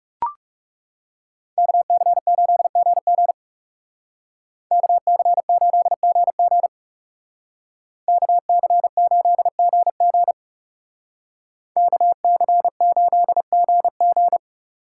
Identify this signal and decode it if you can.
Morse code, heard as a keyed tone, about 35 wpm, 700 Hz. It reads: KC8GG KC8GG KC8GG KC8GG
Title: Speed Racing - USA 2x2 Call Signs 30wpm